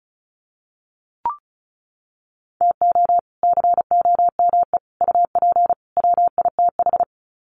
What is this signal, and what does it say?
Morse code, heard as a keyed tone, about 34 wpm, 700 Hz. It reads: TO COME UP WITH
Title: Sets of 4 Words 35wpm